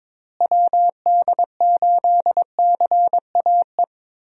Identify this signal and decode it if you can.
Morse code, heard as a keyed tone, about 22 wpm, 700 Hz. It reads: WD8CAE
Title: USA Call Signs - 6 Characters - Repeated 3x - 22wpm